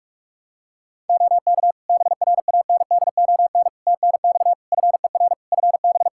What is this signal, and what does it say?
Morse code, heard as a keyed tone, about 46 wpm, 700 Hz. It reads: OK BRANDON TNX FER FB
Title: K1USN SST QSO Practice 45wpm